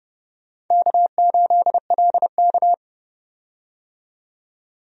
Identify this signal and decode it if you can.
Morse code, heard as a keyed tone, about 30 wpm, 700 Hz. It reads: K8LK
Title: USA Call Signs - 4 Characters - 30wpm